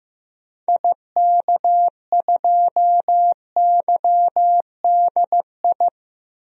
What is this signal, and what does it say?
Morse code, heard as a keyed tone, about 15 wpm, 700 Hz. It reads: IK2YDI
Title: DXCC-100 Call Signs - 6 Characters - Repeated 3x - 15wpm